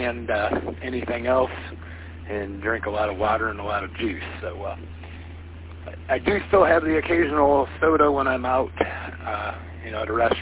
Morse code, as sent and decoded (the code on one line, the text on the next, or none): none